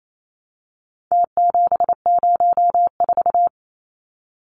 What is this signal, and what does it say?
Morse code, heard as a keyed tone, about 28 wpm, 700 Hz. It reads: T704
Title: Numbers - Contest Style - 4-Digit 28wpm